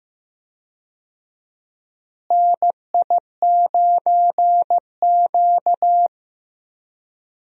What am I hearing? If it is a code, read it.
Morse code, heard as a keyed tone, about 15 wpm, 700 Hz. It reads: NI9Q
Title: USA Call Signs - 4 Characters - Repeated 2x - 15wpm